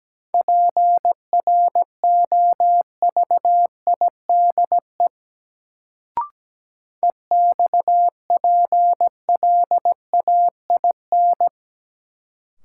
Morse code, heard as a keyed tone, about 17 wpm, 700 Hz. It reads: PROVIDE EXPLAIN
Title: Sentences from Top 1100 Words - New Word Drill Alt - 17wpm